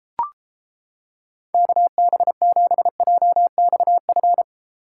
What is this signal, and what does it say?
Morse code, heard as a keyed tone, about 29 wpm, 700 Hz. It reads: KB7JXF